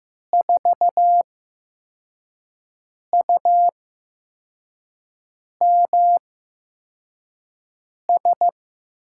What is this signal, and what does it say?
Morse code, heard as a keyed tone, about 15 wpm, 700 Hz. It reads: 4 U M S
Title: USA Call Signs - 8x Character Spacing - 15wpm